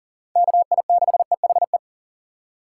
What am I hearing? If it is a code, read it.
Morse code, heard as a keyed tone, about 40 wpm, 700 Hz. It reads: KI6EHE
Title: USA Call Signs - 6 Characters - 4x Character Spacing - 40wpm